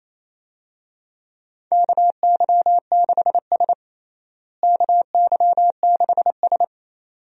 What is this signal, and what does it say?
Morse code, heard as a keyed tone, about 28 wpm, 700 Hz. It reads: KY6S KY6S
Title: USA Call Signs - 4 Characters - Repeated 2x - 28wpm